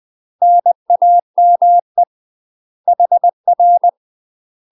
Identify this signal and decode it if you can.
Morse code, heard as a keyed tone, about 20 wpm, 700 Hz. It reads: NAME HR
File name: UK MASTER QSOs TRAINING / 20WPM / UK MASTER QSOs 0022